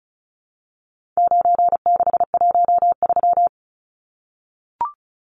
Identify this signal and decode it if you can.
Morse code, heard as a keyed tone, about 34 wpm, 700 Hz. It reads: 9613
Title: Numbers - 4-Digit 35wpm